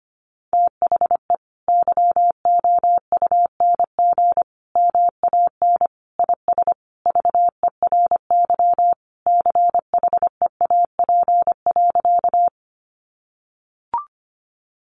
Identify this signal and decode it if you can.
Morse code, heard as a keyed tone, about 25 wpm, 700 Hz. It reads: THE YOUNG MAN IS VERY CHEAP.
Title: Sentences from Top 1200 Words 25wpm